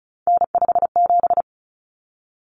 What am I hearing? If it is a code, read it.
Morse code, heard as a keyed tone, about 34 wpm, 700 Hz. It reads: N57
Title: Common Cut Numbers - 3-Digit 35wpm